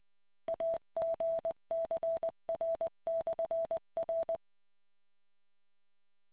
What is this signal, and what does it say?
Morse code, heard as a keyed tone, about 20 wpm, 660 Hz. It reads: AGCR/R